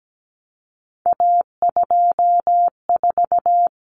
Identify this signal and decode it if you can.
Morse code, heard as a keyed tone, about 17 wpm, 700 Hz. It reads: A24
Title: Common Cut Numbers - 3-Digit 17wpm